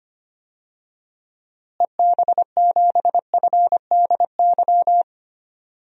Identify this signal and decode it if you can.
Morse code, heard as a keyed tone, about 25 wpm, 700 Hz. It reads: EB7FDY
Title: DXCC-100 Call Signs - 6 Characters - Repeated 3x - 25wpm